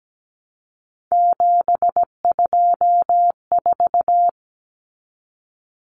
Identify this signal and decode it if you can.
Morse code, heard as a keyed tone, about 17 wpm, 700 Hz. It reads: 724
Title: Numbers - 3-Digit 17wpm